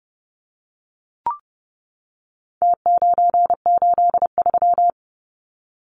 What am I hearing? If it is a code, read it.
Morse code, heard as a keyed tone, about 30 wpm, 700 Hz. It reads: T983
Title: Numbers - Contest Style - 4-Digit 30wpm